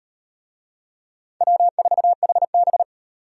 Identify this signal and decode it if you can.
Morse code, heard as a keyed tone, about 34 wpm, 700 Hz. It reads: W4HB